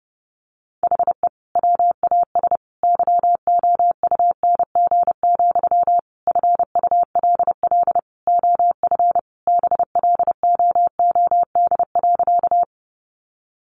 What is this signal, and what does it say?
Morse code, heard as a keyed tone, about 28 wpm, 700 Hz. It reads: HE WAS YOUNG, FULL OF BLOOD.
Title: Sentences from Top 1100 Words 30wpm